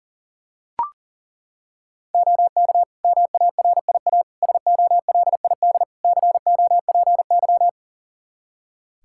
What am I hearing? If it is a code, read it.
Morse code, heard as a keyed tone, about 40 wpm, 700 Hz. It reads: OK MARIA SOLID COPY
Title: Practice QSO - Exchange 3 - More Info - 40wpm